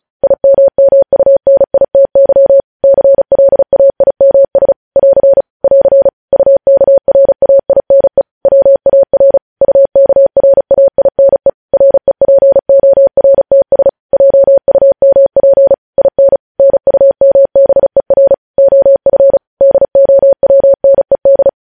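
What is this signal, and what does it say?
Morse code, heard as a keyed tone, about 36 wpm, 550 Hz. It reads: IMMUNITY CLAIMS <AR> <AR> UKRAINE WAR UKRAINE REPORTS JUMP IN NUMBER OF DOWNED
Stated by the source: News headlines sent in Morse code at 35 WPM.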